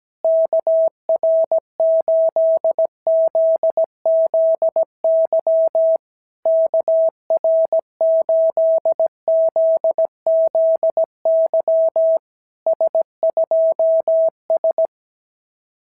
Morse code, heard as a keyed tone, about 17 wpm, 660 Hz. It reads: KR8ZZY KR8ZZY S2S
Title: SOTA QSO Practice 17wpm